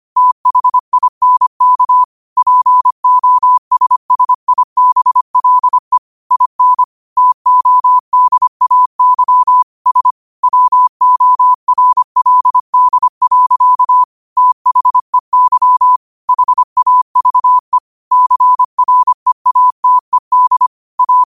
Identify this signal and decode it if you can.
Morse code, heard as a keyed tone, about 25 wpm, 1000 Hz. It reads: THINK POSSIBLE IN TODAY S WORLD. THEY HAVE CREATED A
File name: qotdfiles / 25wpm